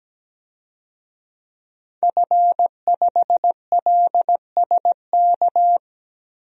Morse code, heard as a keyed tone, about 17 wpm, 700 Hz. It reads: F5LSK
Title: DXCC-100 Call Signs - 5 Characters - 17wpm